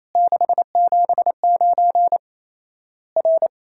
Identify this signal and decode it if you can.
Morse code, heard as a keyed tone, about 28 wpm, 700 Hz. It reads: 679 R
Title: Worked All Europe DX Contest - QTC Exchange - 28wpm